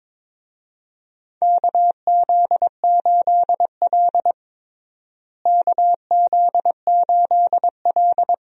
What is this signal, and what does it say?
Morse code, heard as a keyed tone, about 22 wpm, 700 Hz. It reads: KZ8L KZ8L
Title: USA Call Signs - 4 Characters - Repeated 3x - 22wpm